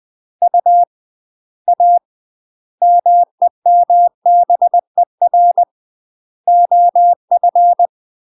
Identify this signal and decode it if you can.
Morse code, heard as a keyed tone, about 20 wpm, 700 Hz. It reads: U A MEMBER OF
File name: UK MASTER QSOs TRAINING/20WPM